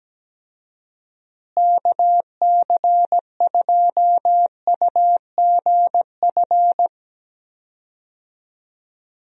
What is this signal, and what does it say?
Morse code, heard as a keyed tone, about 17 wpm, 700 Hz. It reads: KC2UGF
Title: USA Call Signs - 6 Characters - Repeated 2x - 17wpm